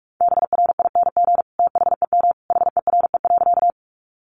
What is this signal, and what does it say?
Morse code, heard as a keyed tone, about 44 wpm, 700 Hz. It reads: BRING THEM HERE.
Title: Sentences from Top 500 Words 45wpm